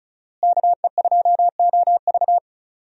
Morse code, heard as a keyed tone, about 36 wpm, 700 Hz. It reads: KE2OV
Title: USA Call Signs - 5 Characters - Repeated 3x - 35wpm